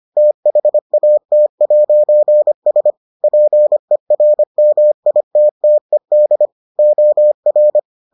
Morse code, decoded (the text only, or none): THAT'S PERMITTED OR